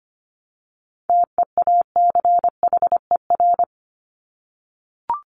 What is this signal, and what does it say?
Morse code, heard as a keyed tone, about 25 wpm, 700 Hz. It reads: TEACHER
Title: Top 751-1000 Words 25wpm